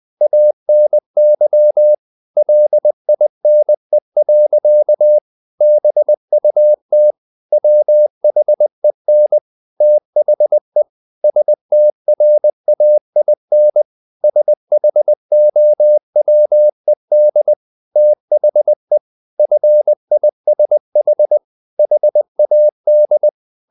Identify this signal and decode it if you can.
Morse code, heard as a keyed tone, about 20 wpm, 600 Hz. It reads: ANY LINE. BUT WHEN THE STRAIN SHOWED THE FISH HAD
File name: the old man and the sea part 2 20wpm